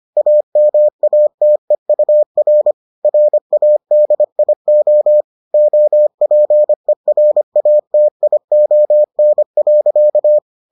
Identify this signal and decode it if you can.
Morse code, heard as a keyed tone, about 25 wpm, 600 Hz. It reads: AMATEUR RADIO OPERATION.